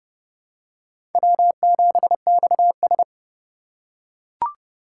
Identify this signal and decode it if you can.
Morse code, heard as a keyed tone, about 29 wpm, 700 Hz. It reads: W7XS
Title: USA Call Signs - 4 Characters - 2x Character Spacing - 30wpm